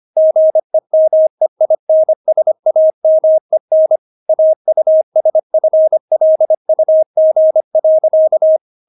Morse code, heard as a keyed tone, about 25 wpm, 630 Hz. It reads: GEMEINSAMEN AUSFLUG.
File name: Witz6 25wpm